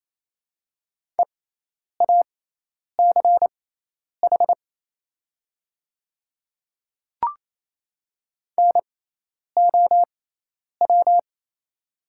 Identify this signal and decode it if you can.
Morse code, heard as a keyed tone, about 27 wpm, spaced as for 10 wpm, 700 Hz. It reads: EACH NOW
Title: 6x Character Spacing - Top 100 Words - 28wpm